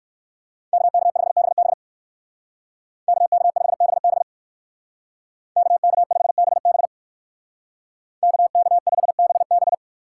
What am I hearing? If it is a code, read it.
Morse code, heard as a keyed tone, about 60 wpm, 700 Hz. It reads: KK5BB KK5BB KK5BB KK5BB